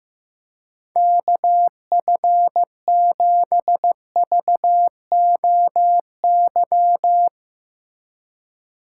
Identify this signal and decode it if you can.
Morse code, heard as a keyed tone, about 15 wpm, 700 Hz. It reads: KF7VOY
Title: USA Call Signs - 4x Character Spacing - 15wpm